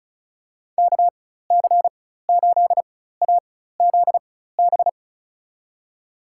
Morse code, heard as a keyed tone, about 36 wpm, spaced as for 16 wpm, 700 Hz. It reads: KC8AZB
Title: USA Call Signs - 6 Characters - 4x Character Spacing - 35wpm